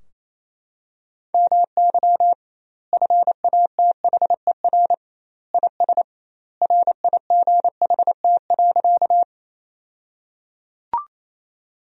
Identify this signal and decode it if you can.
Morse code, heard as a keyed tone, about 28 wpm, 700 Hz. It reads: MY FATHER IS RIGHT.
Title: 2x Word-Spacing - Sentences from Top 300 Words - 28wpm